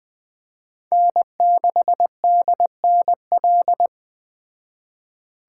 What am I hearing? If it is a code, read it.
Morse code, heard as a keyed tone, about 20 wpm, 700 Hz. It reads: N6DNL
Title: Speed Racing - USA 1x3 Call Signs 17wpm